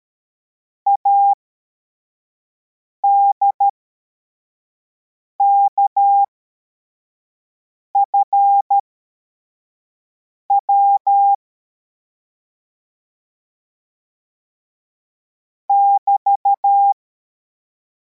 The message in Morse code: ADKFW <BT>